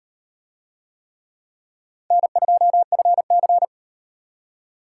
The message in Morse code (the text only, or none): N2FC